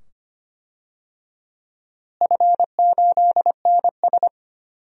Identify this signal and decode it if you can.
Morse code, heard as a keyed tone, about 25 wpm, 700 Hz. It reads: F8NS
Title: DXCC-100 Call Signs - 4 Characters - Repeated 2x - 25wpm